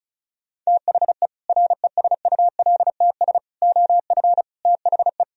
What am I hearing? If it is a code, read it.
Morse code, heard as a keyed tone, about 45 wpm, 700 Hz. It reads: THE RESULTS OF THE